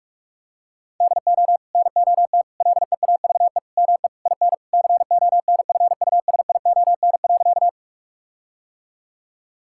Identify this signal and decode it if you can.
Morse code, heard as a keyed tone, about 41 wpm, 700 Hz. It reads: DO NOT LEAVE ME IN CONFUSION.
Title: Sentences from Top 1600 Words 45wpm